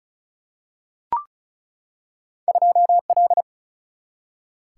Morse code, heard as a keyed tone, about 36 wpm, 700 Hz. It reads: 2 L
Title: Single-Letter Single-Number 35wpm